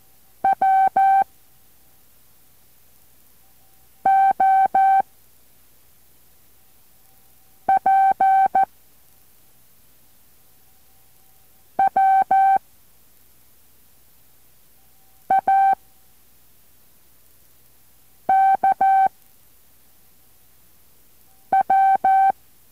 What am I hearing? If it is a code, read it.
Morse code, heard as a keyed tone, about 14 wpm, 770 Hz. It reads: W O P W A K W